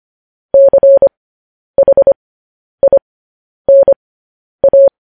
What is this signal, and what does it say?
Morse code, heard as a keyed tone, about 25 wpm, spaced as for 10 wpm, 550 Hz. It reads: CHINA
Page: news headlines in morse code at 10 wpm